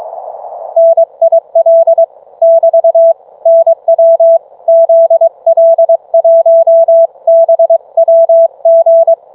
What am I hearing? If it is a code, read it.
Morse code, heard as a keyed tone, about 22 wpm, 660 Hz. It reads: NIL <BT> NW ZL1BWG